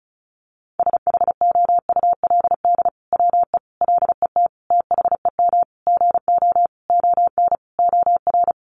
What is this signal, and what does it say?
Morse code, heard as a keyed tone, about 34 wpm, 700 Hz. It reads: SHOULD WE LET THEM GO ON OR
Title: Sentences from Top 600 Words 35wpm